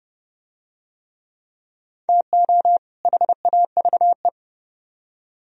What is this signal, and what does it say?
Morse code, heard as a keyed tone, about 30 wpm, 700 Hz. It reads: TO HAVE